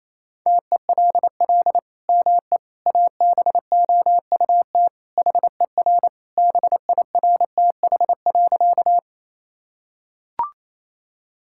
Morse code, heard as a keyed tone, about 28 wpm, 700 Hz. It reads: TELL ME ABOUT HER BIRTH.
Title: Sentences from Top 1000 Words - Review Part 5 - 28wpm